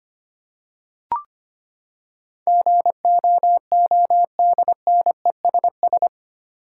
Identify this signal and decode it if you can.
Morse code, heard as a keyed tone, about 25 wpm, 700 Hz. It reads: GOODNESS
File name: Eight Letter Words - 25wpm